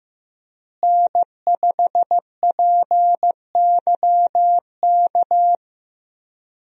Morse code, heard as a keyed tone, about 15 wpm, 700 Hz. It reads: N5PYK
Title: USA Call Signs - 5 Characters - Repeated 2x - 15wpm